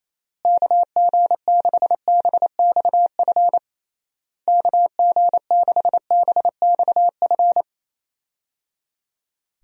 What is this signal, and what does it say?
Morse code, heard as a keyed tone, about 28 wpm, 700 Hz. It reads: KG6BXF KG6BXF
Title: USA Call Signs - 6 Characters - Repeated 3x - 28wpm